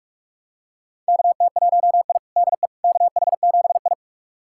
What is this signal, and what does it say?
Morse code, heard as a keyed tone, about 46 wpm, 700 Hz. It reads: KT1I DE KH7I
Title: USA Callsign DE Callsign - 4 Characters - 45wpm